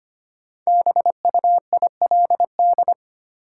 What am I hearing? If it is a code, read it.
Morse code, heard as a keyed tone, about 25 wpm, 700 Hz. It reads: BUILD
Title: Sentences from Top 1100 Words - New Word Drill Alt - 25wpm